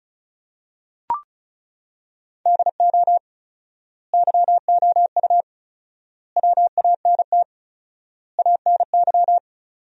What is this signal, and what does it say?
Morse code, heard as a keyed tone, about 36 wpm, 700 Hz. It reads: DO YOU WANT ANY
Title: 4x Word-Spacing - Sentences from Top 400 Words - 35wpm